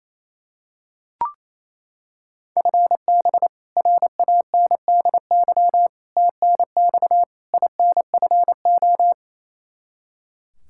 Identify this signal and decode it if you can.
Morse code, heard as a keyed tone, about 28 wpm, 700 Hz. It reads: FB RANDY TNX INFO
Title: K1USN SST QSO Practice 28wpm